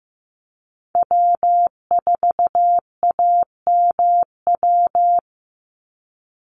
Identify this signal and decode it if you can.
Morse code, heard as a keyed tone, about 15 wpm, 700 Hz. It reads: W4AMW
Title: NA SOTA Chaser Callsigns 15wpm